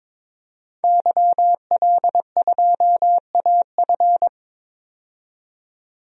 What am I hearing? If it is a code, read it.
Morse code, heard as a keyed tone, about 22 wpm, 700 Hz. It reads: YL2AF
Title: DXCC-100 Call Signs - 5 Characters - 22wpm